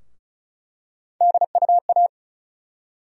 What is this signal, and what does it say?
Morse code, heard as a keyed tone, about 34 wpm, 700 Hz. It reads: DUA
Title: USA Call Sign Postfixes - 3 Letters 35wpm